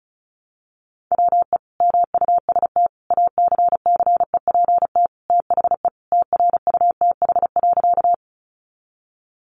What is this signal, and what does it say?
Morse code, heard as a keyed tone, about 34 wpm, 700 Hz. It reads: WE MUST ACCEPT THE TRUTH.
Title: Sentences from Top 1300 Words 35wpm